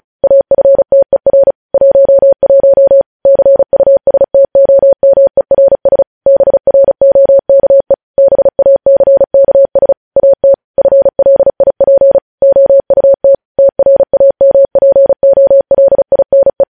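Morse code, heard as a keyed tone, about 36 wpm, 550 Hz. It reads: AFTER 11 CUSTOMERS BROKE BACKS AT FLIP OUT TRAMPOLINE
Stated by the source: News headlines sent in Morse code at 35 WPM.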